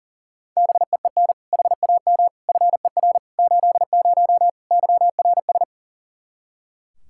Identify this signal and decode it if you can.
Morse code, heard as a keyed tone, about 40 wpm, 700 Hz. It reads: BEEN HAM FER 80 YRS
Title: K1USN SST QSO Practice 40wpm